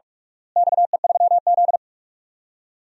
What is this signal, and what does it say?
Morse code, heard as a keyed tone, about 46 wpm, 700 Hz. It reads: XE3Z
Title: DXCC-100 Call Signs - 4 Characters - Repeated 2x - 45wpm